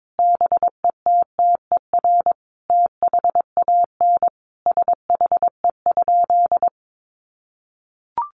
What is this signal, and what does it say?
Morse code, heard as a keyed tone, about 22 wpm, 700 Hz. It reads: BETTER THAN SHE?
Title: Sentences from Top 1200 Words 22wpm